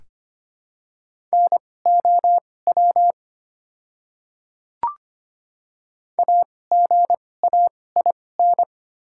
Morse code, heard as a keyed tone, about 25 wpm, spaced as for 18 wpm, 700 Hz. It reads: NOW AGAIN
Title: 2x Character Spacing - Top 300 Words - 25wpm